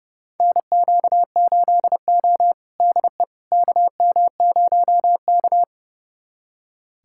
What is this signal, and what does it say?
Morse code, heard as a keyed tone, about 30 wpm, 700 Hz. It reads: NQ8O DE KM0K
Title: USA Callsign DE Callsign - 4 Characters - 30wpm